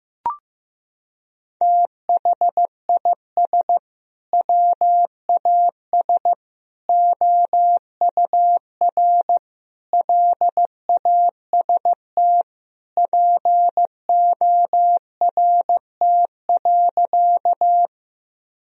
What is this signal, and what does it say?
Morse code, heard as a keyed tone, about 15 wpm, 700 Hz. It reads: THIS WAS OUR LAST PORT.
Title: Sentences from Top 1000 Words - Review Part 1 - 15wpm